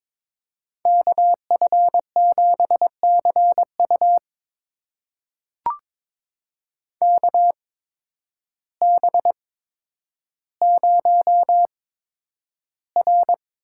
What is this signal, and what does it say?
Morse code, heard as a keyed tone, about 21 wpm, spaced as for 10 wpm, 700 Hz. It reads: KF7CU KB0R
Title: USA Call Signs - 5 Characters - 8x Character Spacing - 22wpm